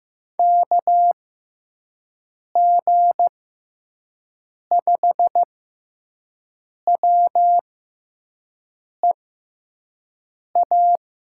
Morse code, heard as a keyed tone, about 15 wpm, spaced as for 5 wpm, 700 Hz. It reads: KG5WEA
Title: USA Call Signs - 6 Characters - 6x Character Spacing - 15wpm